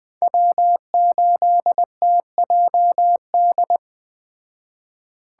Morse code, heard as a keyed tone, about 20 wpm, 700 Hz. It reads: W8TJD
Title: Speed Racing - USA 1x3 Call Signs 17wpm